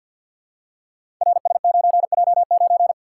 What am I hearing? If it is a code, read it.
Morse code, heard as a keyed tone, about 52 wpm, 700 Hz. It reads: RS9J9
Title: Code Groups - 5 Letter:Numbers - 50wpm